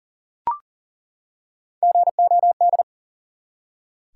Morse code, heard as a keyed tone, about 38 wpm, 700 Hz. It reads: GOD
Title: Three Letter Words - 40wpm